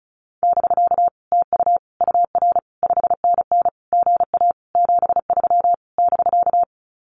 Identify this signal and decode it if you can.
Morse code, heard as a keyed tone, about 34 wpm, 700 Hz. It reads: <BK> TU UR 5NN GA 73 <BK>
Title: Common QSO Phrases Encore 35wpm